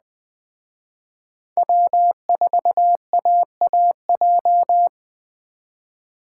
Morse code, heard as a keyed tone, about 21 wpm, 700 Hz. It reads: W4AAJ